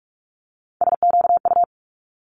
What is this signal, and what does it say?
Morse code, heard as a keyed tone, about 46 wpm, 700 Hz. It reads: SQU
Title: Multi-letter Phonemes Quick Fire 45wpm